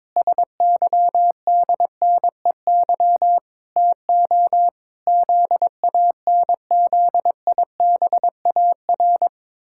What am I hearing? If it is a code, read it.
Morse code, heard as a keyed tone, about 22 wpm, 700 Hz. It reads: SYDNEY TO ZANZIBAR
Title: Pangram Challenge v2 - 22wpm